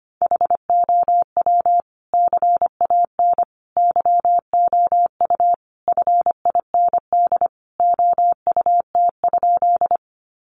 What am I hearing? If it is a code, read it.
Morse code, heard as a keyed tone, about 25 wpm, 700 Hz. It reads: HOW CAN YOU FIND OUT?
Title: Sentences from Top 200 Words 25wpm